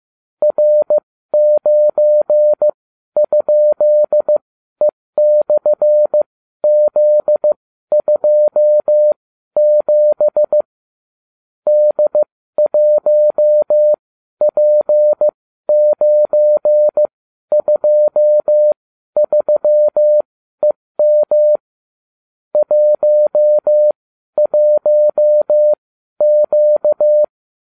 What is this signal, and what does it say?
Morse code, heard as a keyed tone, about 15 wpm, spaced as for 12 wpm, 600 Hz. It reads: R9?E/Z27 D1P923EM 11Q